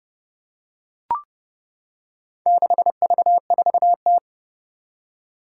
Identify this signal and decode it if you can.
Morse code, heard as a keyed tone, about 30 wpm, 690 Hz. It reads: 6V4T
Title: Four Challenging Letter-Number Combinations 30wpm